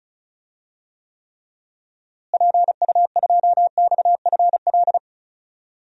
Morse code, heard as a keyed tone, about 34 wpm, 700 Hz. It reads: PU2XFL
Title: DXCC-100 Call Signs - 6 Characters - Repeated 2x - 35wpm